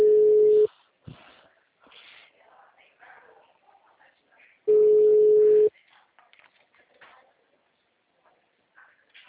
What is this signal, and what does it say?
Ringback tone (Europe, 425 Hz), with speech.
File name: out- -2011- - - .58